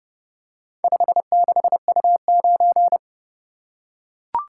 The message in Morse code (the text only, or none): 56U9